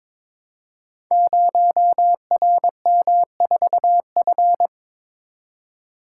Morse code, heard as a keyed tone, about 23 wpm, 700 Hz. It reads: 0RM4F
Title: Code Groups - 5 Letter:Numbers - 22wpm